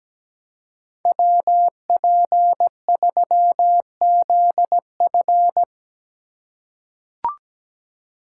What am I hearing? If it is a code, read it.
Morse code, heard as a keyed tone, about 17 wpm, 700 Hz. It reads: WP3ZF
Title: USA Call Signs - 5 Characters - 3x Character Spacing - 17wpm